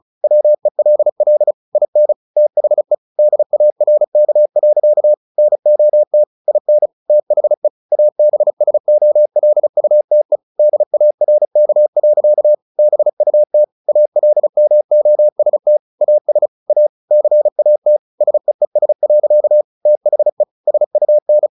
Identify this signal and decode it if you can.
Morse code, heard as a keyed tone, about 34 wpm, 600 Hz. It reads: WELL IN THE DARK. NOT IN THE ABSOLUTE DARK. BUT ALMOST AS A CAT SEES. THE SUN
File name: the old man and the sea / the old man and the sea 1 35wpm